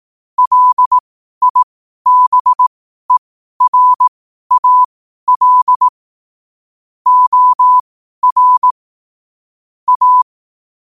Morse code, heard as a keyed tone, about 18 wpm, spaced as for 12 wpm, 1000 Hz. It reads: LIBERAL OR A
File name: qotdfiles / 12wpm